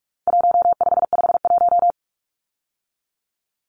Morse code, heard as a keyed tone, about 46 wpm, 700 Hz. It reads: 1551
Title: Numbers - Contest Style - 4-Digit 45wpm